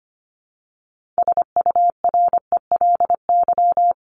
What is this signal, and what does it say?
Morse code, heard as a keyed tone, about 25 wpm, 700 Hz. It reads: SURELY
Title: LY Suffix - Top 100 Words with Base Words 25wpm